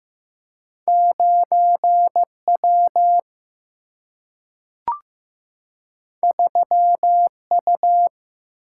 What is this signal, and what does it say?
Morse code, heard as a keyed tone, about 15 wpm, 700 Hz. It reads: 9W 3U
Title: Single-Letter Single-Number 15wpm